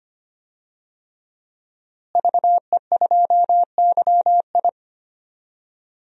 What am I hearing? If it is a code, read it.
Morse code, heard as a keyed tone, about 25 wpm, 700 Hz. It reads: VE2YI
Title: DXCC-100 Call Signs - 5 Characters - Repeated 3x - 25wpm